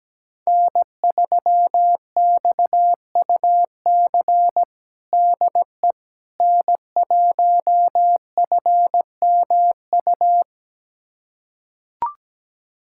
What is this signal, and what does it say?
Morse code, heard as a keyed tone, about 17 wpm, 700 Hz. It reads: N3XUC DE N1FMU
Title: USA Callsign DE Callsign - 5 Characters - 17wpm